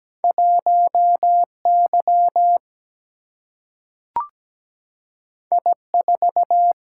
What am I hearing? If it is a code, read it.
Morse code, heard as a keyed tone, about 17 wpm, 700 Hz. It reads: 1 Y I 4
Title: Single-Letter Single-Number 17wpm